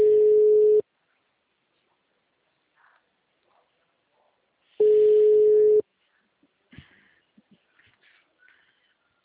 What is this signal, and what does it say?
Ringback tone (Europe, 425 Hz).